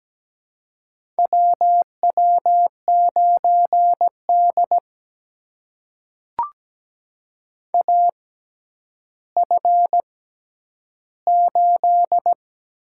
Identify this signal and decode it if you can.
Morse code, heard as a keyed tone, about 17 wpm, spaced as for 10 wpm, 700 Hz. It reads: WW9D AF8
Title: USA Call Signs - 4 Characters - 6x Character Spacing - 17wpm